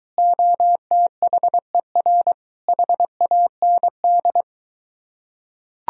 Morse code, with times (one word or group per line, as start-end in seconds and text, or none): 0.2-2.3 OTHER
2.7-4.4 HAND